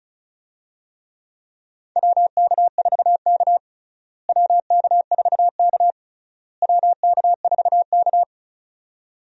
Morse code, heard as a keyed tone, about 36 wpm, 700 Hz. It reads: WK4K WK4K WK4K
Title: USA Call Signs - 4 Characters - Repeated 3x - 35wpm